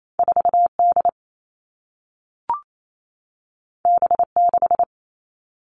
Morse code, heard as a keyed tone, about 28 wpm, 700 Hz. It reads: 4D B6
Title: All Cut Numbers - 2-Digit 28wpm